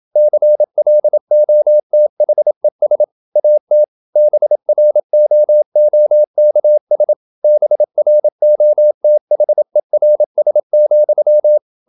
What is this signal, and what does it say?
Morse code, heard as a keyed tone, about 27 wpm, 600 Hz. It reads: CLOTHES AT BROOKS BROTHERS,
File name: Never 0027wpm